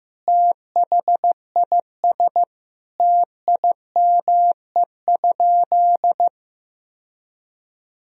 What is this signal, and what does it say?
Morse code, heard as a keyed tone, about 15 wpm, 700 Hz. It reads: THIS TIME?
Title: Sentences from Top 1000 Words - Review Part 3 - 15wpm